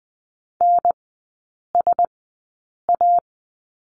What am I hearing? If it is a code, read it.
Morse code, heard as a keyed tone, about 20 wpm, spaced as for 8 wpm, 700 Hz. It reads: NSA
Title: 067 Lesson 12e - Letter d - Three Characters 20wpm